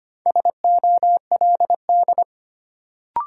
Morse code, heard as a keyed tone, about 25 wpm, 700 Hz. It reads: SOLD
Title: Sentences from Top 1400 Words - New Word Drill Alt - 25wpm